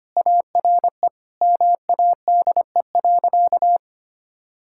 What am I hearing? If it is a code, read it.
Morse code, heard as a keyed tone, about 25 wpm, 700 Hz. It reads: ARE MADE.
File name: practice / 25wpm / Sentences with 4-Word Phrases - 25wpm